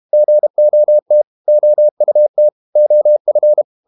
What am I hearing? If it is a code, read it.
Morse code, heard as a keyed tone, about 32 wpm, 600 Hz. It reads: GOT OUT OF
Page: Never 31wpm | CW med Gnister